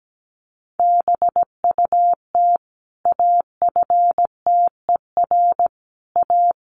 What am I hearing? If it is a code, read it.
Morse code, heard as a keyed tone, about 17 wpm, 700 Hz. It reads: BUT AFTER A
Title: Sentences from Top 1500 Words 17wpm